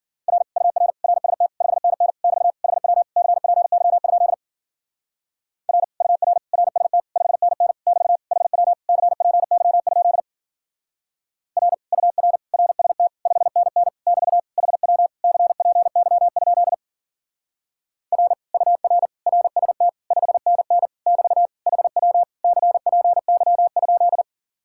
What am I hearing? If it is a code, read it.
Morse code, heard as a keyed tone, about 51 wpm, 700 Hz. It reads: R UR RST 5NN <BT> HW CPY? R UR RST 5NN <BT> HW CPY? R UR RST 5NN <BT> HW CPY? R UR RST 5NN <BT> HW CPY?